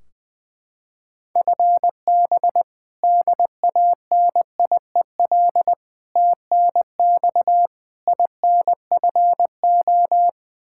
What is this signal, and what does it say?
Morse code, heard as a keyed tone, about 21 wpm, 700 Hz. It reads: FB DANIEL TNX INFO
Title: Practice QSO - Exchange 3 - More Info - 20wpm